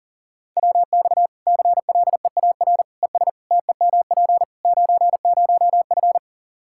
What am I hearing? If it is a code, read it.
Morse code, heard as a keyed tone, about 37 wpm, 700 Hz. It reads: WX CLEAR ES TEMP 90F
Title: Practice QSO - Exchange 2 - Info - 40wpm